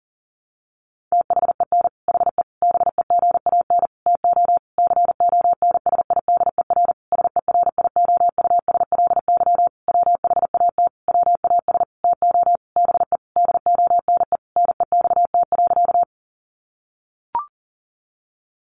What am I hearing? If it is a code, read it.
Morse code, heard as a keyed tone, about 40 wpm, 700 Hz. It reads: THEN HE BEGAN TO CONSIDER SERIOUSLY WHAT WAS TO BE DONE NEXT.
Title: Sentences from Top 1200 Words 40wpm